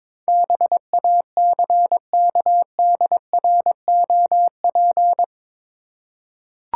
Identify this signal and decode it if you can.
Morse code, heard as a keyed tone, about 22 wpm, 700 Hz. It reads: BACKDROP
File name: practice / 22wpm / Eight Letter Words - 22wpm